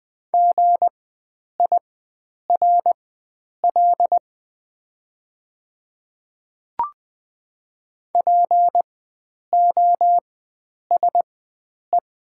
Morse code, heard as a keyed tone, about 20 wpm, spaced as for 9 wpm, 700 Hz. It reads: GIRL POSE
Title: 4x Character Spacing - Top 500 Words - 20wpm